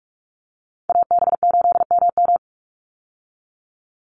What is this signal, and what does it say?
Morse code, heard as a keyed tone, about 44 wpm, 700 Hz. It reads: AB8MM
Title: USA Call Signs 45wpm